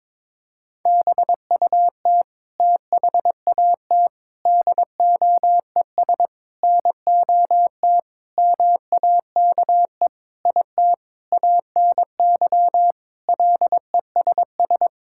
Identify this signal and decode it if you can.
Morse code, heard as a keyed tone, about 22 wpm, 700 Hz. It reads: BUT THAT DOES NOT MAKE IT ANY LESS
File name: Sentences with 7-Word Phrases - 22wpm